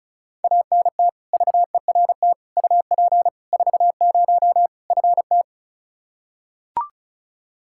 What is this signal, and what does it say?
Morse code, heard as a keyed tone, about 36 wpm, 700 Hz. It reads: ANT VERT UP 40 FT
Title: Practice QSO - Exchange 2 - Info - 35wpm